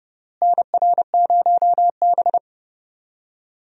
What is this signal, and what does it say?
Morse code, heard as a keyed tone, about 30 wpm, 700 Hz. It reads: NR0B
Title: USA Call Signs - 4 Characters - Repeated 3x - 30wpm